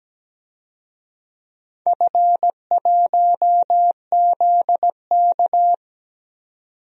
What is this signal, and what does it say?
Morse code, heard as a keyed tone, about 17 wpm, 700 Hz. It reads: F1ZK
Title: DXCC-100 Call Signs - 4 Characters - 17wpm